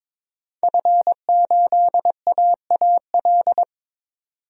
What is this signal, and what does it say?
Morse code, heard as a keyed tone, about 22 wpm, 700 Hz. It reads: F8AAL
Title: DXCC-100 Call Signs - 5 Characters - 22wpm